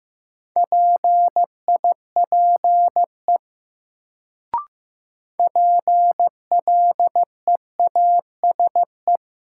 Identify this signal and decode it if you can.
Morse code, heard as a keyed tone, about 15 wpm, 700 Hz. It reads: PIPE PLEASE
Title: Sentences from Top 1500 Words - New Word Drill - 15wpm